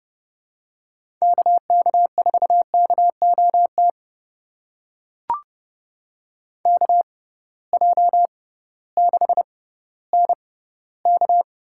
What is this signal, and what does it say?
Morse code, heard as a keyed tone, about 31 wpm, spaced as for 16 wpm, 700 Hz. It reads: KK4KOT KJ6NK
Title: USA Call Signs - 6 Characters - 6x Character Spacing - 30wpm